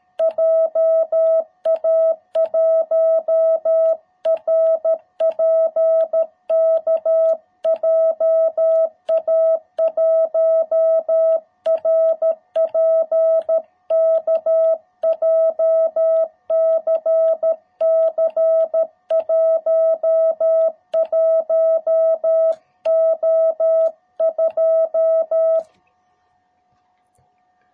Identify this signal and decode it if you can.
Morse code, heard as a keyed tone, about 13 wpm, 660 Hz. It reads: JA1RPK JA1RPK JCC 11O2